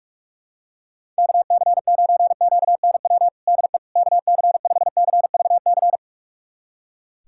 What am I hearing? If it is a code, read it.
Morse code, heard as a keyed tone, about 45 wpm, 700 Hz. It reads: KC9QNW DE KC5CVC
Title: USA Callsign DE Callsign 45wpm